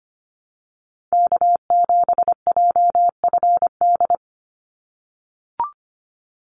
Morse code, heard as a keyed tone, about 25 wpm, 700 Hz. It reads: K7JFD
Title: NA SOTA Chaser Callsigns 25wpm